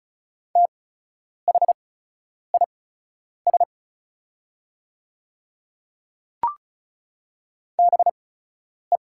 Morse code, heard as a keyed tone, about 36 wpm, spaced as for 10 wpm, 690 Hz. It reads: THIS BE
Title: 8x Character Spacing - Top 500 Words - 35wpm